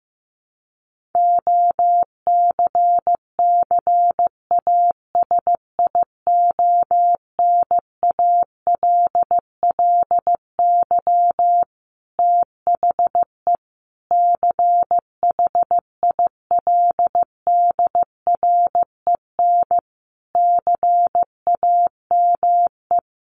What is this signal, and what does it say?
Morse code, heard as a keyed tone, about 15 wpm, 700 Hz. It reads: OCCASIONALLY THE CHILDREN CAME
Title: Sentences from Top 1300 Words 15wpm